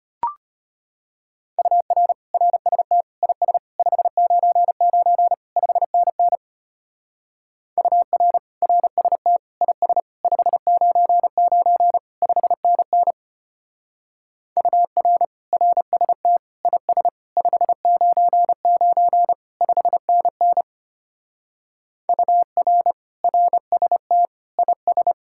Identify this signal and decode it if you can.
Morse code, heard as a keyed tone, about 33 wpm, 700 Hz. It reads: UR RST IS 599 5NN UR RST IS 599 5NN UR RST IS 599 5NN UR RST IS